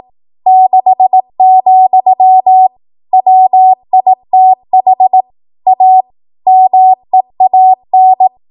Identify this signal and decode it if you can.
Morse code, heard as a keyed tone, about 18 wpm, 750 Hz. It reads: 6, WITH A MEAN
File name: Archive / 18 WPM / 18WPM